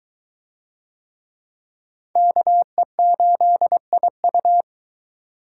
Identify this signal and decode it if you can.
Morse code, heard as a keyed tone, about 22 wpm, 700 Hz. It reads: KE8IU